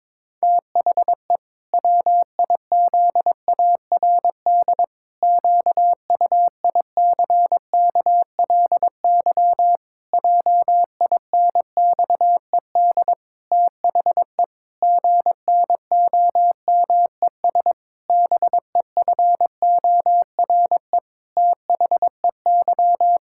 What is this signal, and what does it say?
Morse code, heard as a keyed tone, about 22 wpm, 700 Hz. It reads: THE WIZARD QUICKLY JINXED THE GNOMES BEFORE THEY
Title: Pangram Challenge v2 - 22wpm